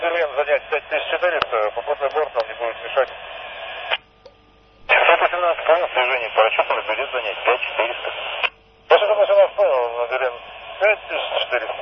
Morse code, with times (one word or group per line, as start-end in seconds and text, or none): none